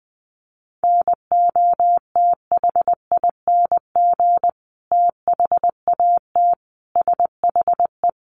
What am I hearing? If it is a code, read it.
Morse code, heard as a keyed tone, about 20 wpm, 700 Hz. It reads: NOTHING THAT SHE
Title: Sentences from Top 1100 Words 20wpm